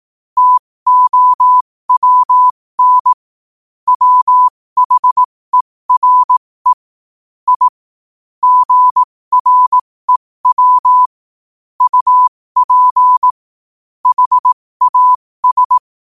Morse code, heard as a keyed tone, about 18 wpm, spaced as for 15 wpm, 1000 Hz. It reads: TOWN WHERE I GREW UP HAS
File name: qotdfiles / 15wpm